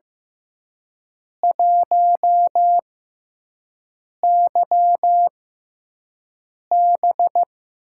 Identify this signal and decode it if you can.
Morse code, heard as a keyed tone, about 15 wpm, spaced as for 5 wpm, 700 Hz. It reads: 1YB
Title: USA Call Signs - 4 Characters - 6x Character Spacing - 15wpm